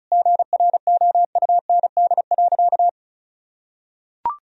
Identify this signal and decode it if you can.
Morse code, heard as a keyed tone, about 34 wpm, 700 Hz. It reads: GROUND.
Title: Sentences from Top 1000 Words - Review Part 4 - 35wpm